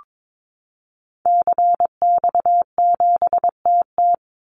Morse code, heard as a keyed tone, about 22 wpm, 700 Hz. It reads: CX7TT
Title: Call Signs Repeated 3x - 22wpm